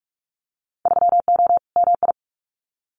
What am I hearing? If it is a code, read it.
Morse code, heard as a keyed tone, about 46 wpm, 700 Hz. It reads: 3O MI
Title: Winter Field Day Exchanges Encore 45wpm